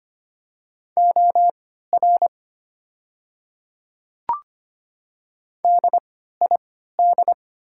Morse code, heard as a keyed tone, about 25 wpm, spaced as for 14 wpm, 700 Hz. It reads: OR DID
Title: 3x Character Spacing - Top 100 Words - 25wpm